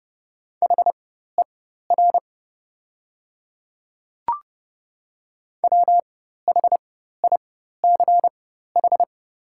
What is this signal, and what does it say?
Morse code, heard as a keyed tone, about 30 wpm, spaced as for 14 wpm, 700 Hz. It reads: HER WHICH
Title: 4x Character Spacing - Top 100 Words - 30wpm